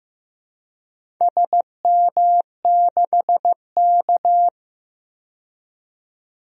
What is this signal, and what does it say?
Morse code, heard as a keyed tone, about 15 wpm, 700 Hz. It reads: SM6K
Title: DXCC-100 Call Signs - 4 Characters - 15wpm